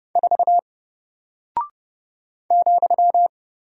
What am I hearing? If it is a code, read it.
Morse code, heard as a keyed tone, about 30 wpm, 700 Hz. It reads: 4 ,
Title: Single Character (Letter:Number:Punctuation:Prosign) Alt 30wpm